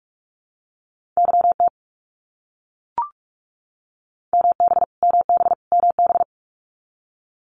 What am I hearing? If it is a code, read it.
Morse code, heard as a keyed tone, about 44 wpm, 700 Hz. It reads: YT MB MB MB
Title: Canadian Provinces and Territories - Repeated 3x - 45wpm